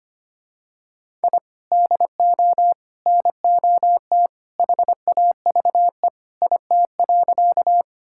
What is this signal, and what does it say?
Morse code, heard as a keyed tone, about 25 wpm, 700 Hz. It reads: I DO NOT HAVE IT.
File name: practice / 25wpm / Sentences with 2-Word Phrases - 25wpm